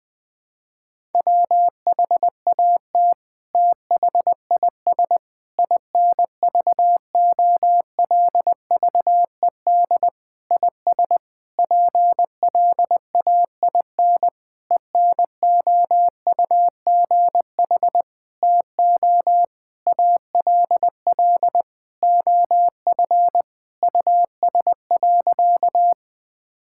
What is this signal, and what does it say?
Morse code, heard as a keyed tone, about 21 wpm, 700 Hz. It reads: WHAT THIS INVOLVED IS PLAIN ENOUGH TO ALL OF US.
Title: Sentences from Top 1000 Words - Review Part 5 - 20wpm